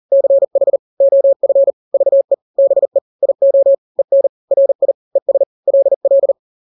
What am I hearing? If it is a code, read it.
Morse code, heard as a keyed tone, about 40 wpm, 550 Hz. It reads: CH OF VE BE IO EN RI ES LL
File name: n-grams / 2-grams 40 35 WPM